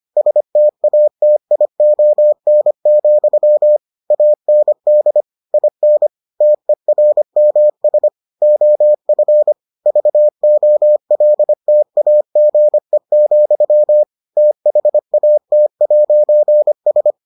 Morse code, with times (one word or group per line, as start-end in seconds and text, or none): 0.2-3.8 STATION,
4.1-5.2 AND
5.5-6.1 IN
6.4-8.1 TERMS
8.4-9.5 OF
9.9-14.0 VOLTAGE,
14.4-17.1 THAT'S